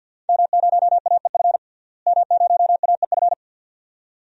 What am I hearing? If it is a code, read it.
Morse code, heard as a keyed tone, about 52 wpm, 700 Hz. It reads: M0AEF M0AEF
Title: DXCC-100 Call Signs - 5 Characters - Repeated 2x - 50wpm